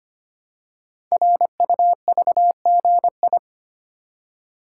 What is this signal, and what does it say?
Morse code, heard as a keyed tone, about 25 wpm, 700 Hz. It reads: RUVGI
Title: Code Groups - 5 Letter:Numbers - 25wpm